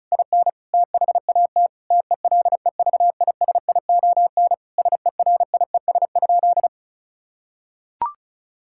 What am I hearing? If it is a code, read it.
Morse code, heard as a keyed tone, about 34 wpm, 700 Hz. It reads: IN THAT TELEVISION SERIES?
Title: Sentences from Top 1000 Words - Review Part 5 - 35wpm